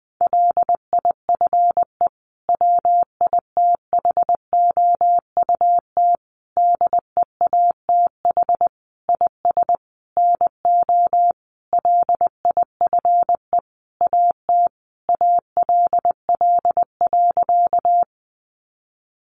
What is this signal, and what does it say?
Morse code, heard as a keyed tone, about 20 wpm, 700 Hz. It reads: LIFE WITHOUT DEATH IS NO LIFE AT ALL.
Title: Sentences from Top 1100 Words 20wpm